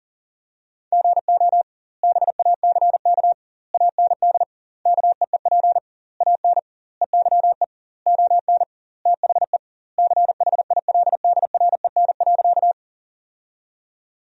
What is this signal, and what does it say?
Morse code, heard as a keyed tone, about 40 wpm, 700 Hz. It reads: GO BACK AND KEEP AN EYE ON THE CHILDREN.
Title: 2x Word-Spacing - Sentences from Top 300 Words - 40wpm